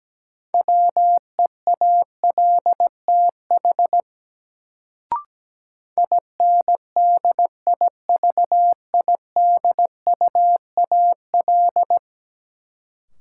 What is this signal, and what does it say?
Morse code, heard as a keyed tone, about 17 wpm, 700 Hz. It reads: WEALTH INDIVIDUAL
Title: Sentences from Top 1000 Words - Review Part 4 - New Word Drill - 17wpm